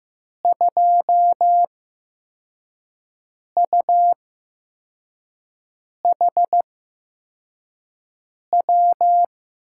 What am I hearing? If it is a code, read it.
Morse code, heard as a keyed tone, about 15 wpm, spaced as for 4 wpm, 700 Hz. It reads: 2UHW
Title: USA Call Signs - 8x Character Spacing - 15wpm